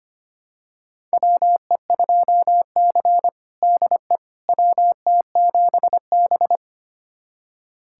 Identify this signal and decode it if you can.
Morse code, heard as a keyed tone, about 25 wpm, 700 Hz. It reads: WE2C DE WT7B
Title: USA Callsign DE Callsign - 4 Characters - 25wpm